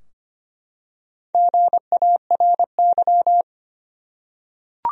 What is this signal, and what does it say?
Morse code, heard as a keyed tone, about 25 wpm, 700 Hz. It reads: GARY
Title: USA Common Names - 25wpm